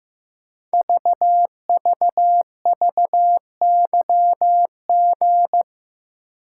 Morse code, heard as a keyed tone, about 15 wpm, 700 Hz. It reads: VVVYG
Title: Code Groups - 5 Letters - 15wpm